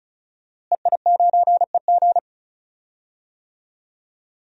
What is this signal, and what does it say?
Morse code, heard as a keyed tone, about 36 wpm, 700 Hz. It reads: EI9EG
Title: DXCC-100 Call Signs - 5 Characters - Repeated 3x - 35wpm